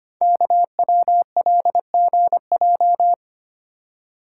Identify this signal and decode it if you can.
Morse code, heard as a keyed tone, about 25 wpm, 700 Hz. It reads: KWLGJ
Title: Code Groups - 5 Letters - 25wpm